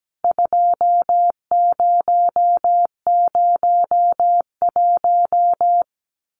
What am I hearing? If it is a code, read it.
Morse code, heard as a keyed tone, about 17 wpm, 700 Hz. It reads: 2001
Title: CWT Exchange 17wpm